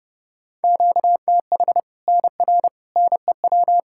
Morse code, heard as a keyed tone, about 32 wpm, 700 Hz. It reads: QTH NR NEW
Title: K1USN SST QSO Practice 30wpm